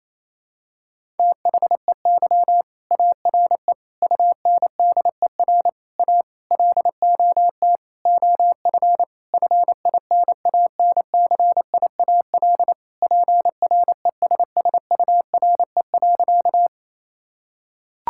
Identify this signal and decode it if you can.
Morse code, heard as a keyed tone, about 28 wpm, 700 Hz. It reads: THEY ARE UNDER A LOT OF FINANCIAL PRESSURE.
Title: Sentences from Top 1000 Words - Review Part 6 - 28wpm